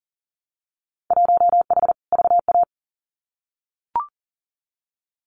Morse code, heard as a keyed tone, about 40 wpm, 700 Hz. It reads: 1H VA
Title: Winter Field Day Exchanges 40wpm